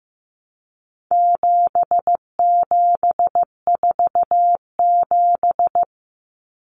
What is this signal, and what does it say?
Morse code, heard as a keyed tone, about 15 wpm, 700 Hz. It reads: 7747
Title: Common Cut Numbers - 4-Digit 15wpm